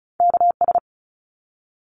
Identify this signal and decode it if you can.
Morse code, heard as a keyed tone, about 34 wpm, 700 Hz. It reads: KS
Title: ARRL Sections 35wpm